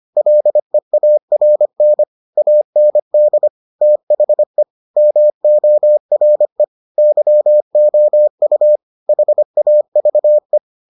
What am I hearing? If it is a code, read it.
Morse code, heard as a keyed tone, about 25 wpm, 600 Hz. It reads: LEARN AND THE MORE YOU HAVE